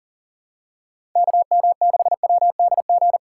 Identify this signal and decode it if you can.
Morse code, heard as a keyed tone, about 40 wpm, 700 Hz. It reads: KM6WDG
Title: USA Call Signs - 6 Characters - 4x Character Spacing - 40wpm